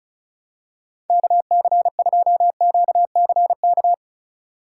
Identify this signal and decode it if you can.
Morse code, heard as a keyed tone, about 35 wpm, 700 Hz. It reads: KC2QCK
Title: USA Call Signs - 6 Characters - 3x Character Spacing - 35wpm